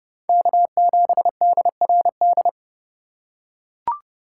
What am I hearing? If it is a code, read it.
Morse code, heard as a keyed tone, about 31 wpm, 700 Hz. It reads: K7DRD
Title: USA Call Signs - 3x Character Spacing - 30wpm